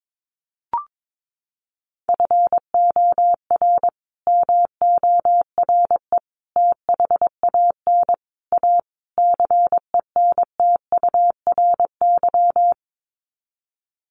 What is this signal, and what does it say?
Morse code, heard as a keyed tone, about 22 wpm, 700 Hz. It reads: FOR MORE THAN A CENTURY
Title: Sets of 5 Words Encore2 22wpm